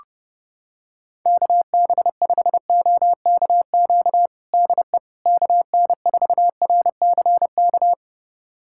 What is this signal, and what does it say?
Morse code, heard as a keyed tone, about 30 wpm, 700 Hz. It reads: KB5OKQ DE KN4RCK
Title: USA Callsign DE Callsign 30wpm